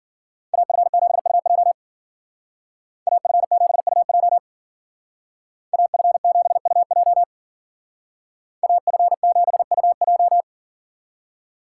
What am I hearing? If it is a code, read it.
Morse code, heard as a keyed tone, about 49 wpm, 700 Hz. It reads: AF7UJ AF7UJ AF7UJ AF7UJ
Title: Speed Racing - USA 2x2 Call Signs 40wpm